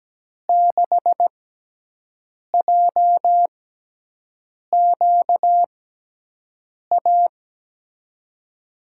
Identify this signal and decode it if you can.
Morse code, heard as a keyed tone, about 17 wpm, spaced as for 6 wpm, 700 Hz. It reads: 6JQA
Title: USA Call Signs - 5 Characters - 6x Character Spacing - 17wpm